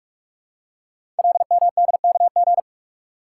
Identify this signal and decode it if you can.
Morse code, heard as a keyed tone, about 44 wpm, 700 Hz. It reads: LMDKG
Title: Code Groups - 5 Letters - 45wpm